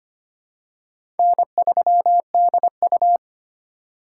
Morse code, heard as a keyed tone, about 25 wpm, 700 Hz. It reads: N3DU
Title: USA Call Signs - 4 Characters - 2x Character Spacing - 25wpm